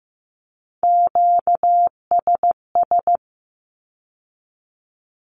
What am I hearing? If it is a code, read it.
Morse code, heard as a keyed tone, about 15 wpm, 700 Hz. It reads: QSS
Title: Q-Codes with Spoken Meaning - 15wpm